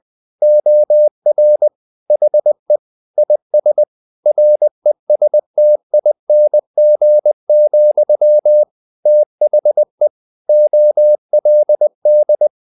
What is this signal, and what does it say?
Morse code, heard as a keyed tone, about 20 wpm, 600 Hz. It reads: OR HE IS RESTING, THE OLD
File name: the old man and the sea part 2 20wpm